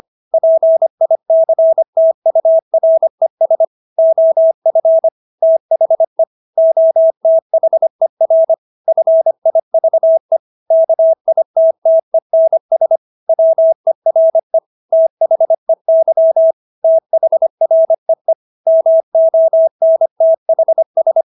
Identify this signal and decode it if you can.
Morse code, heard as a keyed tone, about 25 wpm, 650 Hz. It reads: PICTURES OF THE OTHER FIVE KITTENS WERE THEY THREE MONTHS